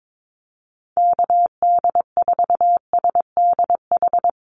Morse code, heard as a keyed tone, about 22 wpm, 700 Hz. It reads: KD4SDH
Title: USA Call Signs Repeated 2x 22wpm